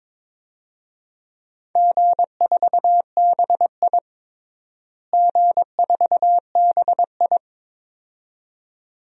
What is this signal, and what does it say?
Morse code, heard as a keyed tone, about 23 wpm, 700 Hz. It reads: G4BI G4BI
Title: DXCC-100 Call Signs - 4 Characters - Repeated 2x - 22wpm